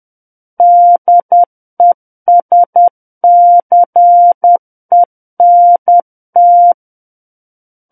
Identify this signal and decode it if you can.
Morse code, heard as a keyed tone, about 10 wpm, 700 Hz. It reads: DESCENT
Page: War of the Worlds - 01-Chapter 1 - 10 WPM